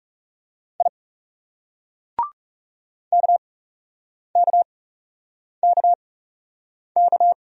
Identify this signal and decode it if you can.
Morse code, heard as a keyed tone, about 37 wpm, spaced as for 8 wpm, 700 Hz. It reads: I KKKK
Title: Speed-Racing Single Letters 30wpm